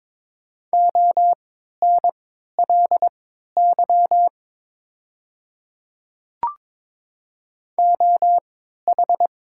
Morse code, heard as a keyed tone, about 22 wpm, spaced as for 13 wpm, 700 Hz. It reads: ONLY OH
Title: 3x Character Spacing - Top 500 Words - 22wpm